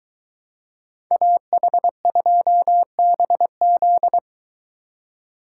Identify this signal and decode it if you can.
Morse code, heard as a keyed tone, about 23 wpm, 700 Hz. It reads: AH2BZ